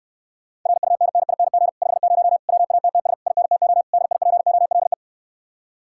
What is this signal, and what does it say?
Morse code, heard as a keyed tone, about 69 wpm, 700 Hz. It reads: RUNNING 50 WATTS INTO DIPOLE